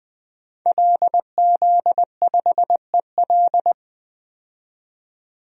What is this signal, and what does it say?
Morse code, heard as a keyed tone, about 20 wpm, 700 Hz. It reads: LZ5EL
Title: DXCC-100 Call Signs - 5 Characters - 20wpm